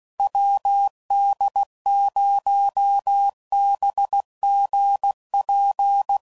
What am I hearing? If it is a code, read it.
Morse code, heard as a keyed tone, about 16 wpm, 800 Hz. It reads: WD0BGP
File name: CW Code Groups @15WPM / QSOs / WD0BGP Spaced